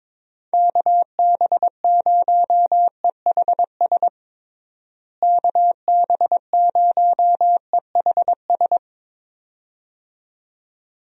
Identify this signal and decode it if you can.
Morse code, heard as a keyed tone, about 21 wpm, 700 Hz. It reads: KB0EHS KB0EHS
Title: USA Call Signs - 6 Characters - Repeated 2x - 22wpm